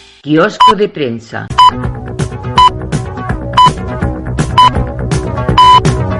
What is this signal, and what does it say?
Radio time signal (time pips), with speech. It is heard from 0.6 to 5.8 s.